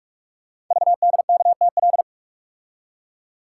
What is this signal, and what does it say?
Morse code, heard as a keyed tone, about 44 wpm, 700 Hz. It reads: VDKTL
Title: Code Groups - 5 Letters - 45wpm